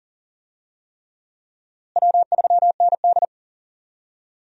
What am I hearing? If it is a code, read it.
Morse code, heard as a keyed tone, about 40 wpm, 700 Hz. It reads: W3ND
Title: USA Call Signs - 4 Characters - Repeated 3x - 40wpm